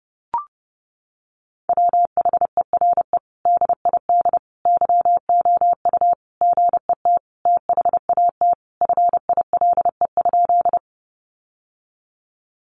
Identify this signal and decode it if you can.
Morse code, heard as a keyed tone, about 30 wpm, 700 Hz. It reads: WHERE DID YOU GET THAT FILE?
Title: Sentences from Top 1200 Words 30wpm